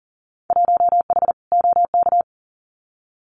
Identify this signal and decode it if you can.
Morse code, heard as a keyed tone, about 40 wpm, 700 Hz. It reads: 1H OK
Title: Winter Field Day Exchanges 40wpm